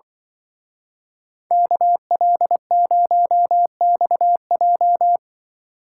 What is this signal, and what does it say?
Morse code, heard as a keyed tone, about 23 wpm, 700 Hz. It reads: KL0XJ
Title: Speed Racing - USA 2x2 Call Signs 20wpm